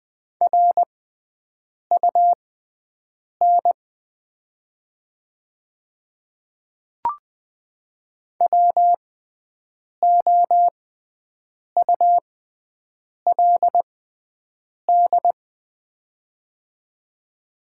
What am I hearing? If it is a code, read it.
Morse code, heard as a keyed tone, about 20 wpm, spaced as for 7 wpm, 700 Hz. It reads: RUN WOULD
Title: 6x Character Spacing - Top 500 Words - 20wpm